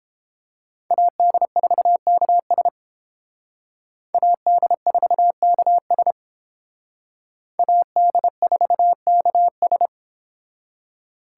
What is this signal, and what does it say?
Morse code, heard as a keyed tone, about 29 wpm, 700 Hz. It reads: AD4KS AD4KS AD4KS